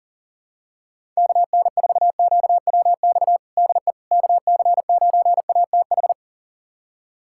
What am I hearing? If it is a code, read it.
Morse code, heard as a keyed tone, about 40 wpm, 700 Hz. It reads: KN4QWX DE KC9ATH
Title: USA Callsign DE Callsign - 6 Characters - 40wpm